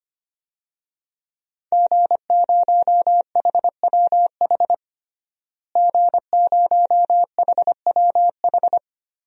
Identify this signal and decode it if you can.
Morse code, heard as a keyed tone, about 25 wpm, 700 Hz. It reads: G0HWH G0HWH
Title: DXCC-100 Call Signs - 5 Characters - Repeated 3x - 25wpm